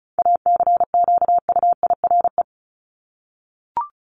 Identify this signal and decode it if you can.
Morse code, heard as a keyed tone, about 34 wpm, 700 Hz. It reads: ACQUIRE
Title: Top 1501-1750 Words 35wpm